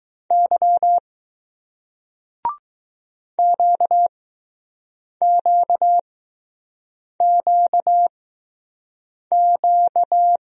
Morse code, heard as a keyed tone, about 20 wpm, spaced as for 6 wpm, 700 Hz. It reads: Y QQQQ